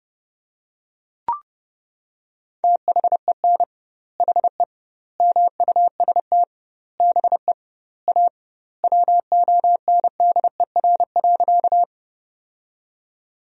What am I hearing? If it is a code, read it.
Morse code, heard as a keyed tone, about 32 wpm, 700 Hz. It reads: THEN HE MUST BE A WONDER.
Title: 2x Word-Spacing - Sentences from Top 500 Words - 30wpm